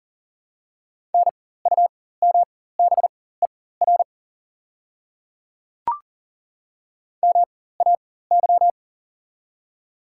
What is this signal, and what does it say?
Morse code, heard as a keyed tone, about 40 wpm, spaced as for 19 wpm, 700 Hz. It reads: NUMBER MAY
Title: 4x Character Spacing - Top 100 Words - 40wpm